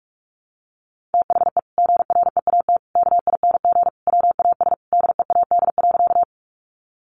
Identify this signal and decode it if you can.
Morse code, heard as a keyed tone, about 44 wpm, 700 Hz. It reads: THE GREAT KING WAS DEAD.
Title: Sentences from Top 1500 Words 45wpm